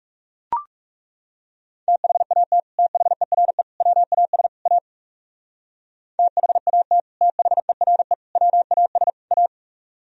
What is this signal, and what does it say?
Morse code, heard as a keyed tone, about 37 wpm, 700 Hz. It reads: THAT THERE WAS A THAT THERE WAS A
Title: Speed-Racing - Sets of 4 Words 30wpm